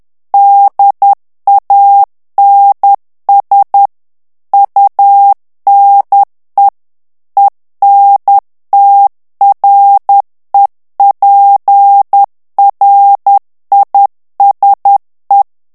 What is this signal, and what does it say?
Morse code, heard as a keyed tone, about 10 wpm, 800 Hz. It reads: DANS UNE ENTREPRISE
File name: morse mp3 / FRANCAIS TEXTE CLAIR 2 VITESSE 10 MOTS MINUTES